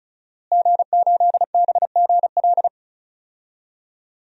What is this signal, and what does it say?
Morse code, heard as a keyed tone, about 34 wpm, 700 Hz. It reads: G8BGL
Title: DXCC-100 Call Signs - 5 Characters - Repeated 3x - 35wpm